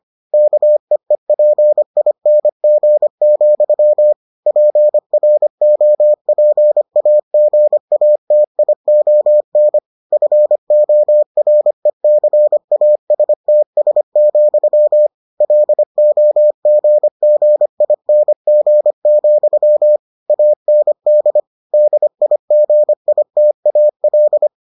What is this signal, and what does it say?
Morse code, heard as a keyed tone, about 25 wpm, 600 Hz. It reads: KEEPING, PROPAGATION FORECASTS, LOGGING, AND DIGITAL